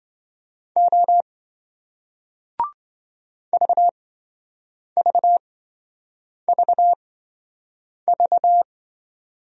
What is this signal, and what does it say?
Morse code, heard as a keyed tone, about 27 wpm, spaced as for 7 wpm, 700 Hz. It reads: O VVVV